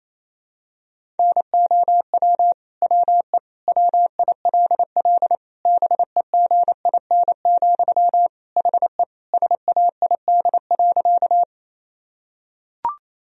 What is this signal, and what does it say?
Morse code, heard as a keyed tone, about 27 wpm, 700 Hz. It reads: NOW WE WILL BEGIN, HE SAID.
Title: Sentences from Top 1000 Words - Review Part 1 - 28wpm